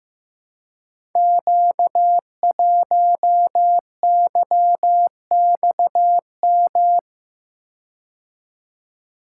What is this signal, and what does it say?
Morse code, heard as a keyed tone, about 15 wpm, 700 Hz. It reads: Q1YXM
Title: Code Groups - 5 Letter:Numbers - 15wpm